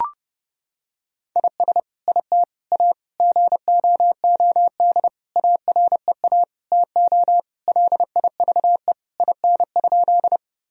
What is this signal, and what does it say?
Morse code, heard as a keyed tone, about 30 wpm, 700 Hz. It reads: IS IT A GOOD AREA TO LIVE IN?
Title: Sentences from Top 1000 Words - Review Part 2 - 30wpm